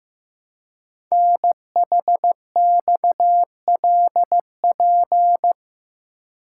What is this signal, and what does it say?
Morse code, heard as a keyed tone, about 15 wpm, 700 Hz. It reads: NHXLP
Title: Code Groups - 5 Letter:Numbers - 15wpm